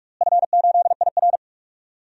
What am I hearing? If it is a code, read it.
Morse code, heard as a keyed tone, about 43 wpm, 700 Hz. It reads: F8IR
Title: DXCC-100 Call Signs - 4 Characters - Repeated 2x - 45wpm